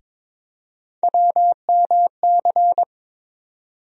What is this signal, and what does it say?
Morse code, heard as a keyed tone, about 22 wpm, 700 Hz. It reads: WMC
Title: USA Call Sign Postfixes - 3 Letters 22wpm